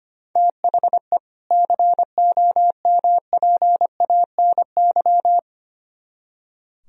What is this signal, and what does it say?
Morse code, heard as a keyed tone, about 31 wpm, 700 Hz. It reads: THE COMPANY
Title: Speed-Racing - Sets of 2 Words 25wpm